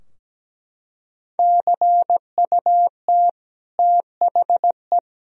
Morse code, heard as a keyed tone, about 17 wpm, 700 Hz. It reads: CUT THE
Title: Sentences from Top 1000 Words - Review Part 3 - 17wpm